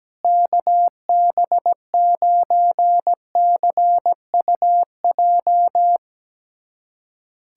Morse code, heard as a keyed tone, about 17 wpm, 700 Hz. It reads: KB9CUJ
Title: USA Callsign DE Callsign - 6 Characters - 17wpm